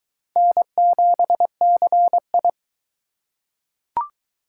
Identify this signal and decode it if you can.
Morse code, heard as a keyed tone, about 20 wpm, 700 Hz. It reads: N7CI